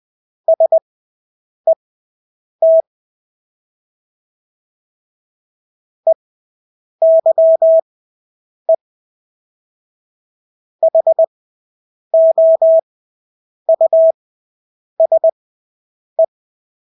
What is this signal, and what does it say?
Morse code, heard as a keyed tone, about 20 wpm, spaced as for 8 wpm, 650 Hz. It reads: SET EYE HOUSE
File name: Engelske ord 20/8 wpm